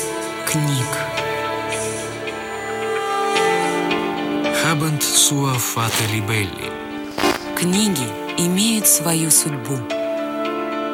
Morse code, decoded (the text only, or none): none